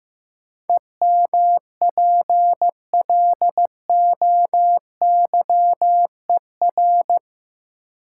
Morse code, heard as a keyed tone, about 15 wpm, 700 Hz. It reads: EMPLOYER
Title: Sentences from Top 1600 Words - New Word Drill Alt - 15wpm